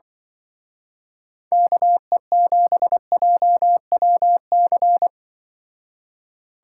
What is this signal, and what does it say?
Morse code, heard as a keyed tone, about 24 wpm, 700 Hz. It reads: KE7JWC
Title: Speed Racing - USA 2x3 Call Signs 20wpm